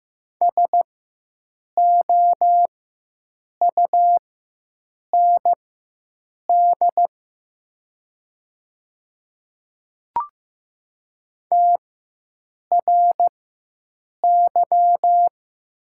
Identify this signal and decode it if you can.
Morse code, heard as a keyed tone, about 15 wpm, spaced as for 7 wpm, 700 Hz. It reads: SOUND TRY
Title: 4x Character Spacing - Top 200 Words - 15wpm